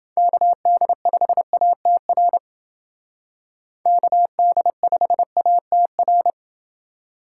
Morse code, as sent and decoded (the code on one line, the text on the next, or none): -.- -.. ..... .- - .-. / -.- -.. ..... .- - .-.
KD5ATR KD5ATR